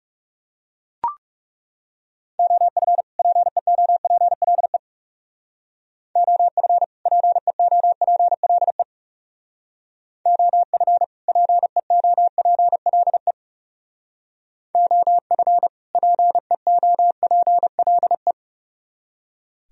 Morse code, heard as a keyed tone, about 35 wpm, 700 Hz. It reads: OF PEOPLE OF PEOPLE OF PEOPLE OF PEOPLE
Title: Speed-Racing - Sets of 2 Words 30wpm